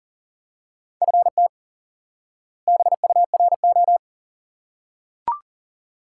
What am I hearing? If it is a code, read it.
Morse code, heard as a keyed tone, about 40 wpm, 700 Hz. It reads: FT BURO
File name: IQE Course 3 - Lesson 89 - YAGI - Two Elements - 40wpm